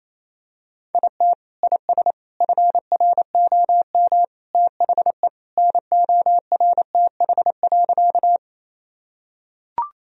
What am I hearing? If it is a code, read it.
Morse code, heard as a keyed tone, about 28 wpm, 700 Hz. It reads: IT IS FROM THE NORTH.
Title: Sentences from Top 1000 Words - Review Part 1 - 28wpm